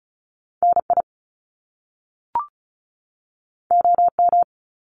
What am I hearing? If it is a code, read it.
Morse code, heard as a keyed tone, about 36 wpm, 700 Hz. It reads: NI OM
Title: Top Two Letter Combinations in Words 35wpm